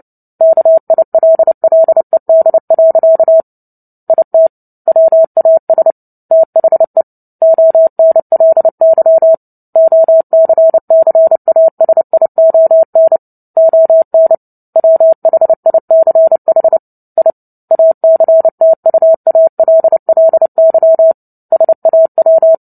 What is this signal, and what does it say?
Morse code, heard as a keyed tone, about 30 wpm, 650 Hz. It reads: KILLED. IT WAS THE ONLY OCCASION ON WHICH I ACTUALLY SAW
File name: Book Two / 03-Chapter 3 - 29 WPM